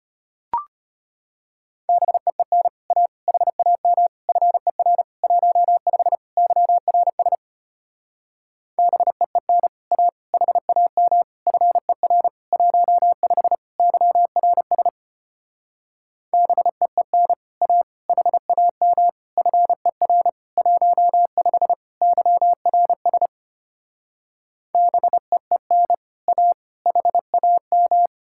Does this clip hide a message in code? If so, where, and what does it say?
Yes: Morse code from 1.9 to 28.1 s, heard as a keyed tone, about 33 wpm, 700 Hz. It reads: BEEN A HAM FER 15 YRS BEEN A HAM FER 15 YRS BEEN A HAM FER 15 YRS BEEN A HAM